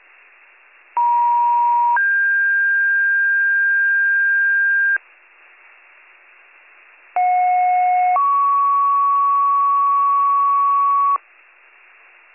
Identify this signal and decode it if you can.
Two-tone page, heard as a one-second tone, then a three-second tone, about 950 then 1600 Hz, about 730 then 1120 Hz. It is heard from 1.0 to 11.2 s.